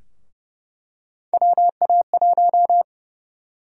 Morse code, heard as a keyed tone, about 30 wpm, 700 Hz. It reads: WA1
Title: USA Call Sign Prefixes - 30wpm